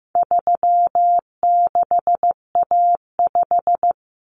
Morse code, heard as a keyed tone, about 15 wpm, 700 Hz. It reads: 36A5
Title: Common Cut Numbers - 4-Digit 15wpm